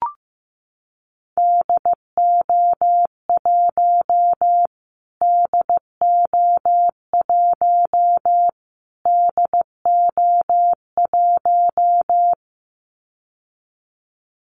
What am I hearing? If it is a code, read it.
Morse code, heard as a keyed tone, about 15 wpm, 700 Hz. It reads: DO1 DO1 DO1
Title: Callsign Prefixes for Top 25 DXCC - Repeated 3x - 15wpm